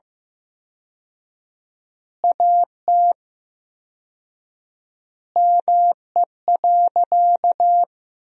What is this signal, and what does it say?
Morse code, heard as a keyed tone, about 15 wpm, 700 Hz. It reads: AT ME.
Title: 4x Word-Spacing - Sentences from Top 200 Words - 15wpm